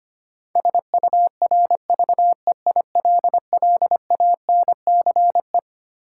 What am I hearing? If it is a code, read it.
Morse code, heard as a keyed tone, about 25 wpm, 700 Hz. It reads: SURVEILLANCE
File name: Twelve Letter Words - 25wpm